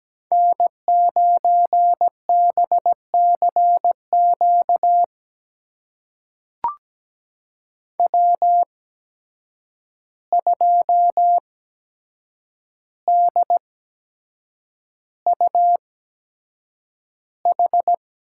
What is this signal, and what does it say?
Morse code, heard as a keyed tone, about 17 wpm, spaced as for 7 wpm, 700 Hz. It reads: N9BCQ W2DUH
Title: USA Call Signs - 5 Characters - 8x Character Spacing - 17wpm